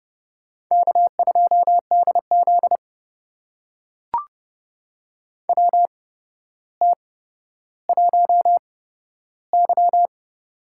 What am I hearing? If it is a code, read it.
Morse code, heard as a keyed tone, about 30 wpm, spaced as for 13 wpm, 700 Hz. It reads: K2DZ WT1Y
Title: USA Call Signs - 8x Character Spacing - 30wpm